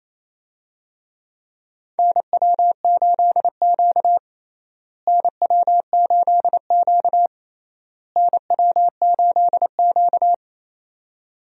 Morse code, heard as a keyed tone, about 28 wpm, 700 Hz. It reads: NW8Q NW8Q NW8Q
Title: USA Call Signs - 4 Characters - Repeated 3x - 28wpm